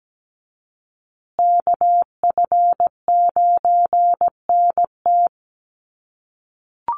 Morse code, heard as a keyed tone, about 17 wpm, 700 Hz. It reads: KF9NT
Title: USA Call Signs Repeated 2x 17wpm